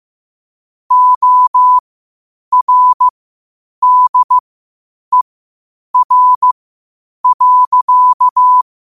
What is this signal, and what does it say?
Morse code, heard as a keyed tone, about 15 wpm, spaced as for 8 wpm, 1000 Hz. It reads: ORDER.
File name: qotdfiles / 8wpm